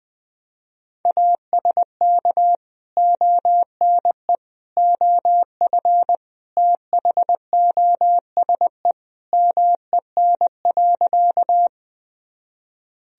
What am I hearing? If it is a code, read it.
Morse code, heard as a keyed tone, about 20 wpm, 700 Hz. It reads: ASK ONE OF THOSE MEN.
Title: Sentences from Top 1000 Words - Review Part 1 - 20wpm